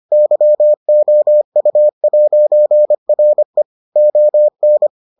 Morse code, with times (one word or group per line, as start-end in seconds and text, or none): 0.1-3.6 YOU'RE
4.0-4.9 ON